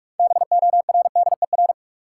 Morse code, heard as a keyed tone, about 46 wpm, 700 Hz. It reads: BORDER
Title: Six Letter Words - 45wpm